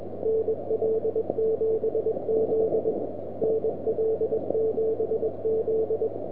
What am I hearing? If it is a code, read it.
Morse code, heard as a keyed tone, about 21 wpm, 440 Hz, with noise. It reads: NL7Z NL7Z
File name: Recordings / nl7z 160 1740z